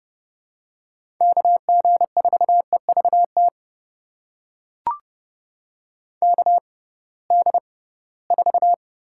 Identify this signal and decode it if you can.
Morse code, heard as a keyed tone, about 30 wpm, spaced as for 19 wpm, 700 Hz. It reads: KG4EVT KD4
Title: USA Call Signs - 6x Character Spacing - 30wpm